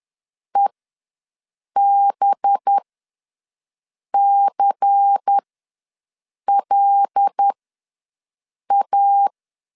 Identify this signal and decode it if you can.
Morse code, heard as a keyed tone, about 10 wpm, spaced as for 5 wpm, 800 Hz. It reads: EBCLA
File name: LEZIONI / 6 PKX0 / 6 PKX0 ws 10wpm 4220